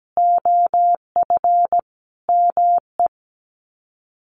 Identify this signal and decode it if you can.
Morse code, heard as a keyed tone, about 17 wpm, 700 Hz. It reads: OF ME
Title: Sets of 4 Words 17wpm